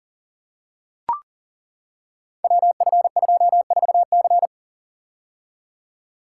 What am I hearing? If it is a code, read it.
Morse code, heard as a keyed tone, about 38 wpm, 700 Hz. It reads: WF24C
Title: Code Groups - 5 Letter:Numbers - 40wpm